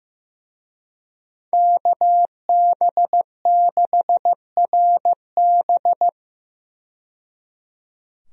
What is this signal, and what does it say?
Morse code, heard as a keyed tone, about 15 wpm, 700 Hz. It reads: KB6RB
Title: USA Call Signs - 5 Characters - 15wpm